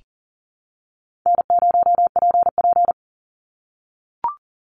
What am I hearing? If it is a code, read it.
Morse code, heard as a keyed tone, about 40 wpm, 700 Hz. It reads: N0PP
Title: Call Signs - 40wpm